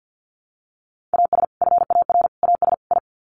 Morse code, heard as a keyed tone, about 48 wpm, 700 Hz. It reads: AS FAR AS I
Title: Sets of 4 Words 50wpm